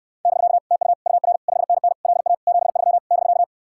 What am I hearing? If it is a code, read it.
Morse code, heard as a keyed tone, about 66 wpm, 700 Hz. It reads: <BK> TU UR 5NN LA 73 <BK>